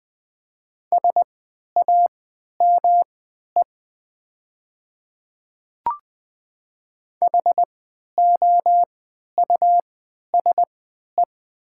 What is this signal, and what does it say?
Morse code, heard as a keyed tone, about 20 wpm, spaced as for 11 wpm, 700 Hz. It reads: SAME HOUSE
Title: 3x Character Spacing - Top 200 Words - 20wpm